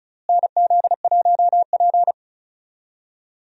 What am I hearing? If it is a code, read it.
Morse code, heard as a keyed tone, about 34 wpm, 700 Hz. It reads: NZ1P
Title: USA Call Signs - 4 Characters - Repeated 2x - 35wpm